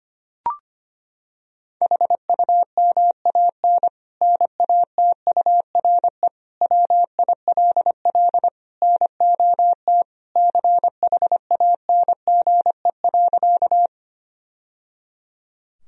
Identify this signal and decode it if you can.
Morse code, heard as a keyed tone, about 25 wpm, 700 Hz. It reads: HUMAN NATURE WILL NOT CHANGE.
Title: Sentences from Top 1000 Words - Review Part 3 - 25wpm